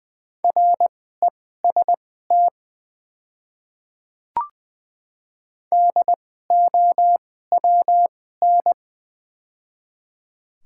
Morse code, heard as a keyed tone, about 20 wpm, spaced as for 14 wpm, 700 Hz. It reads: REST DOWN
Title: 2x Character Spacing - Top 500 Words - 20wpm